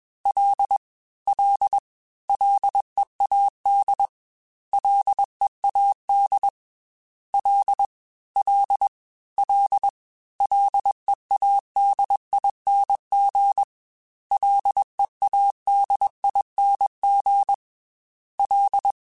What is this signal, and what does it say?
Morse code, heard as a keyed tone, about 21 wpm, 800 Hz. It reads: L L LEAD LEAD L L L LEADING LEADING L
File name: Ls / Single L m3s / Single L Prefixe 3x2 Twice @20